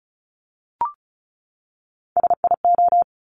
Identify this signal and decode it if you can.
Morse code, heard as a keyed tone, about 36 wpm, 700 Hz. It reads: SIO
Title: Top Three Letter Combinations in Words 35wpm